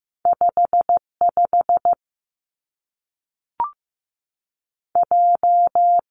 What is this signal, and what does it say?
Morse code, heard as a keyed tone, about 15 wpm, 700 Hz. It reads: 5 5 J
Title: Two Challenging Letter-Number Combinations 15wpm